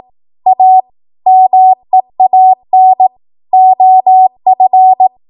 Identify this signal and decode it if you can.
Morse code, heard as a keyed tone, about 18 wpm, 750 Hz. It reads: A MEAN OF
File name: Morse / 18WPM